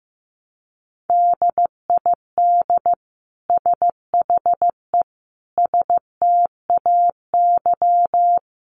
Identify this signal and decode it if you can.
Morse code, heard as a keyed tone, about 15 wpm, 700 Hz. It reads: DID SHE STAY
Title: Sentences from Top 500 Words 15wpm